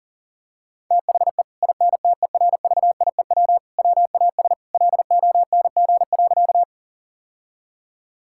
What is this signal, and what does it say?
Morse code, heard as a keyed tone, about 40 wpm, 700 Hz. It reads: THE INTERVIEW WAS LONG.
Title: Sentences from Top 1600 Words 40wpm